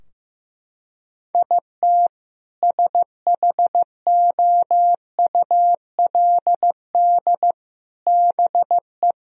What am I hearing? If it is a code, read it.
Morse code, heard as a keyed tone, about 15 wpm, 700 Hz. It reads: IT SHOULD BE
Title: Sentences from Top 1000 Words - Review Part 4 - 15wpm